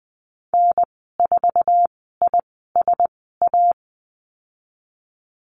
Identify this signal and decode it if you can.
Morse code, heard as a keyed tone, about 20 wpm, 700 Hz. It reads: N 4 I S A
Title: 048 Lesson 9g - Number 4 - Call Signs 20wpm